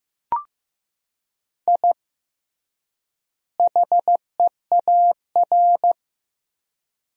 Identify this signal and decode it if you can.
Morse code, heard as a keyed tone, about 15 wpm, 700 Hz. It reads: I HEAR
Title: 3x Word-Spacing - Sentences from Top 500 Words - 15wpm